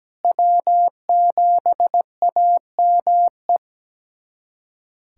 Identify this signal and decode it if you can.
Morse code, heard as a keyed tone, about 17 wpm, 700 Hz. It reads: W7AME
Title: USA Callsign DE Callsign - 5 Characters - 17wpm